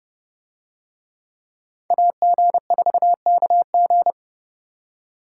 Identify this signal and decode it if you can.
Morse code, heard as a keyed tone, about 28 wpm, 700 Hz. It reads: AG4KG